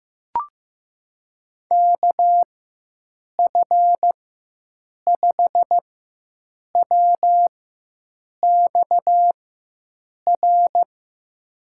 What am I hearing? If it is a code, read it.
Morse code, heard as a keyed tone, about 15 wpm, spaced as for 7 wpm, 700 Hz. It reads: KF5WXR
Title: USA Call Signs - 6 Characters - 4x Character Spacing - 15wpm